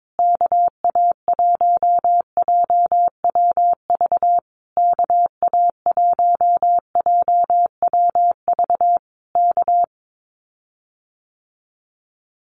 Morse code, heard as a keyed tone, about 22 wpm, 700 Hz. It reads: KA1JWV KA1JWV K
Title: Common QSO Phrases Encore 22wpm